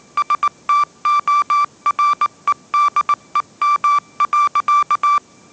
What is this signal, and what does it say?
Morse code, heard as a keyed tone, about 19 wpm, 1170 Hz. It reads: STOREDEM.